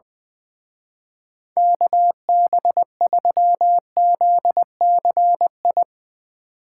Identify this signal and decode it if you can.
Morse code, heard as a keyed tone, about 20 wpm, 700 Hz. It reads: KB3ZCI
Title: Speed Racing - USA 2x3 Call Signs 20wpm